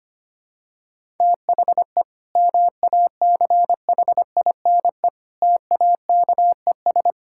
Morse code, heard as a keyed tone, about 25 wpm, 700 Hz. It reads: THE MACHINE TAKES
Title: Sentences from Top 1000 Words - Review Part 6 - 25wpm